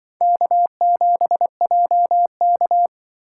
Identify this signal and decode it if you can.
Morse code, heard as a keyed tone, about 23 wpm, 700 Hz. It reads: K7JK
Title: Speed Racing - USA 1x2 Call Signs 20wpm